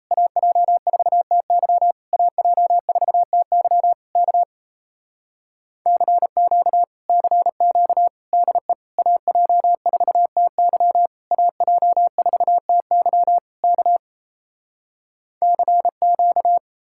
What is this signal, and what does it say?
Morse code, heard as a keyed tone, about 37 wpm, 700 Hz. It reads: AJ4TY AJ4TY K CQ CQ DE AJ4TY AJ4TY K CQ